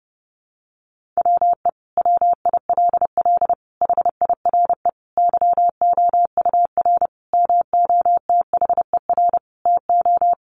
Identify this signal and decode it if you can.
Morse code, heard as a keyed tone, about 30 wpm, 700 Hz. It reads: WE WILL HIRE YOUR MOTHER TO
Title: Sentences from Top 1400 Words 30wpm